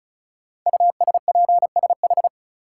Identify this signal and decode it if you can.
Morse code, heard as a keyed tone, about 36 wpm, 700 Hz. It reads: USPSH
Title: Code Groups - 5 Letters - 35wpm